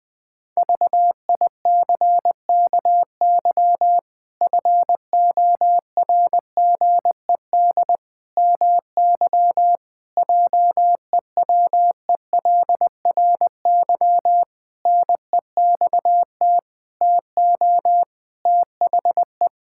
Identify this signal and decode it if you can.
Morse code, heard as a keyed tone, about 21 wpm, 700 Hz. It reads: VICKY FORGED MY JEWELRY NEXT TO THE
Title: Pangram Challenge v2 - 20wpm